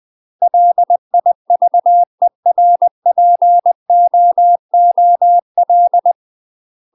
Morse code, heard as a keyed tone, about 20 wpm, 700 Hz. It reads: LIVERPOOL
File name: UK MASTER QSOs TRAINING / 20WPM / UK MASTER QSOs 0005